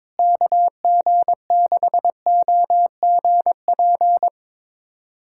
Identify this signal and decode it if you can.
Morse code, heard as a keyed tone, about 22 wpm, 700 Hz. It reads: KG6OGP
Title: USA Callsign DE Callsign - 6 Characters - 22wpm